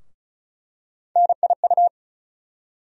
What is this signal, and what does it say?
Morse code, heard as a keyed tone, about 36 wpm, 700 Hz. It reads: NIU
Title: USA Call Sign Postfixes - 3 Letters 35wpm